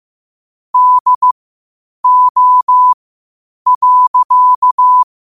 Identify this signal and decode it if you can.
Morse code, heard as a keyed tone, about 15 wpm, spaced as for 8 wpm, 1000 Hz. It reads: DO.
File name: qotdfiles / 8wpm